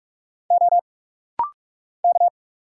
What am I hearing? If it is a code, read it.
Morse code, heard as a keyed tone, about 45 wpm, 700 Hz. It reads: O K
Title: Single Letters - Rapid-Fire 45wpm